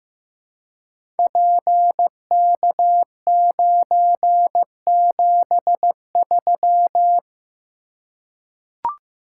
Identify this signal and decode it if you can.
Morse code, heard as a keyed tone, about 15 wpm, 700 Hz. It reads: PK973
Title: Code Groups - 5 Letter:Numbers - 15wpm